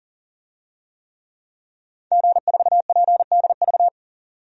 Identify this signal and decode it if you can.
Morse code, heard as a keyed tone, about 40 wpm, 700 Hz. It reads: G4PDV
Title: DXCC-100 Call Signs - 5 Characters - 40wpm